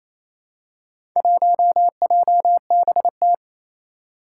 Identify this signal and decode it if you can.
Morse code, heard as a keyed tone, about 29 wpm, 700 Hz. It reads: 1JBT
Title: Four Challenging Letter-Number Combinations 28wpm